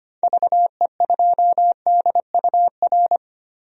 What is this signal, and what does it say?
Morse code, heard as a keyed tone, about 25 wpm, 700 Hz. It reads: VE2DUR
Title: DXCC-100 Call Signs - 6 Characters - 25wpm